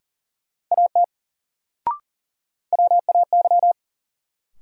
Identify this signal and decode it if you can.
Morse code, heard as a keyed tone, about 40 wpm, 700 Hz. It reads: AT WAY
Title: Sentences from Top 100 Words - New Word Drill - 40wpm